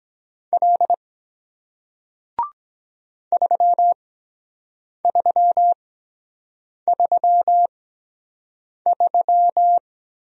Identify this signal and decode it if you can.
Morse code, heard as a keyed tone, about 23 wpm, spaced as for 7 wpm, 700 Hz. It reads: L 3333